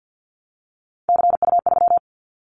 Morse code, heard as a keyed tone, about 50 wpm, 700 Hz. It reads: CU3
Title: Callsign Prefixes for Top 50 DXCC - 50wpm